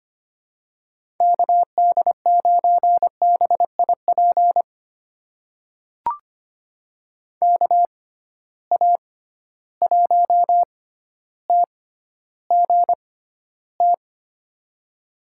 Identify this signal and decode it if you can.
Morse code, heard as a keyed tone, about 25 wpm, spaced as for 13 wpm, 700 Hz. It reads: KD9BIP KA1TGT
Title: USA Call Signs - 6 Characters - 6x Character Spacing - 25wpm